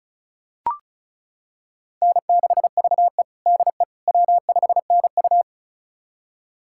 Morse code, heard as a keyed tone, about 36 wpm, 700 Hz. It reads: N6VE DE W5NU
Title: USA Callsign DE Callsign - 4 Characters - 35wpm